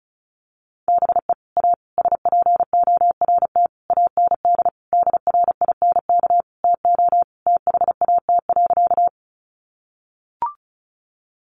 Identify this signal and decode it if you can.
Morse code, heard as a keyed tone, about 34 wpm, 700 Hz. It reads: BE A SPORT AND DRINK TO THAT.
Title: Sentences from Top 1500 Words 35wpm